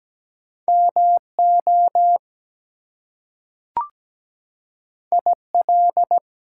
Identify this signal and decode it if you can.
Morse code, heard as a keyed tone, about 17 wpm, 700 Hz. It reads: MO IL
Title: US States and Canadian Provinces 17wpm